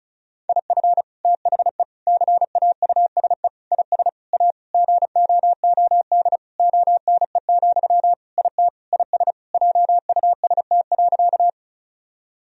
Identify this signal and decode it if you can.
Morse code, heard as a keyed tone, about 36 wpm, 700 Hz. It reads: IF THE CAUSE IS A GOOD ONE, IT IS JUST.
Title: Sentences from Top 1000 Words - Review Part 1 - 35wpm